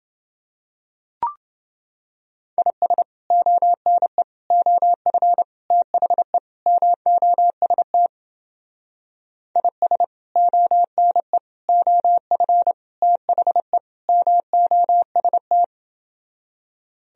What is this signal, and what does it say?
Morse code, heard as a keyed tone, about 30 wpm, 700 Hz. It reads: IS ONE OF THE MOST IS ONE OF THE MOST